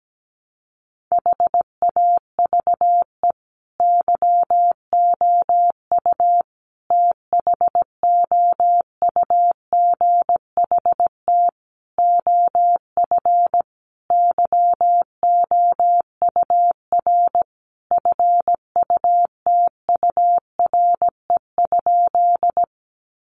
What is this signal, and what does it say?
Morse code, heard as a keyed tone, about 17 wpm, 700 Hz. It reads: HAVE YOU THOUGHT OF YOUR FUTURE?
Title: Sentences from Top 700 Words 17wpm